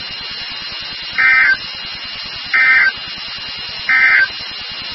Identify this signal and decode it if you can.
Emergency Alert System (SAME) tones, heard as harsh buzzing bursts. It is heard from 1.1 to 4.3 s.
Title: NOAA Weather Radio WXK36